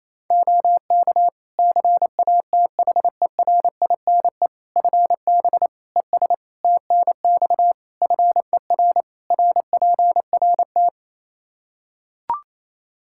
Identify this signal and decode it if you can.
Morse code, heard as a keyed tone, about 28 wpm, 700 Hz. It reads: OK CATHERINE FB ES TNX FER RPRT
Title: Practice QSO - Exchange 2 - Info - 28wpm